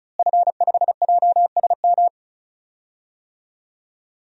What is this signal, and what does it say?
Morse code, heard as a keyed tone, about 34 wpm, 700 Hz. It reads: F5JSM
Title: DXCC-100 Call Signs - 5 Characters - Repeated 3x - 35wpm